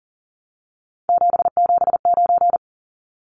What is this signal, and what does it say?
Morse code, heard as a keyed tone, about 40 wpm, 700 Hz. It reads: 779
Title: Numbers - 3-Digit 40wpm